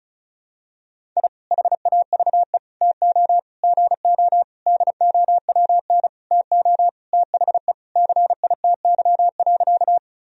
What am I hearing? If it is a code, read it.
Morse code, heard as a keyed tone, about 36 wpm, 700 Hz. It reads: I HAVE TO GO DOWN TO THE CITY.
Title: Sentences from Top 1000 Words - Review Part 1 - 35wpm